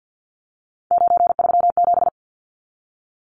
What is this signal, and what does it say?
Morse code, heard as a keyed tone, about 50 wpm, 700 Hz. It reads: 937
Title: Numbers - 3-Digit 50wpm